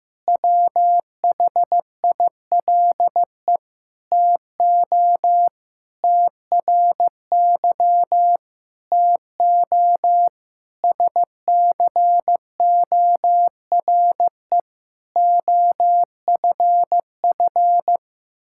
Morse code, heard as a keyed tone, about 15 wpm, 700 Hz. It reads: WHILE TO TRY TO SCORE OFF
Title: Sentences from Top 1600 Words 15wpm